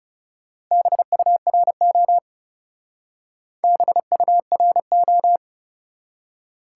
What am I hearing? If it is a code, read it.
Morse code, heard as a keyed tone, about 37 wpm, 700 Hz. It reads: BURO BURO
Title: Speed-Racing Top 180 QSO Elements 30wpm